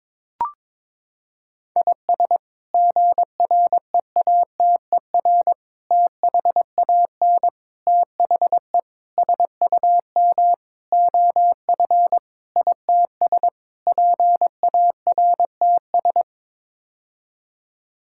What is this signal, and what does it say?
Morse code, heard as a keyed tone, about 22 wpm, 700 Hz. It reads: IS GREATER THAN THE SUM OF ITS PARTS
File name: practice / 22wpm / Sets of 8 Words - 22wpm